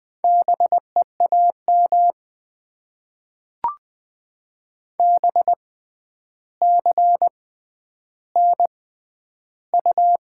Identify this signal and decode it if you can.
Morse code, heard as a keyed tone, about 20 wpm, spaced as for 10 wpm, 700 Hz. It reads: BEAM BCNU
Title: Top 90 QSO Elements - 6x Character Spacing - 20wpm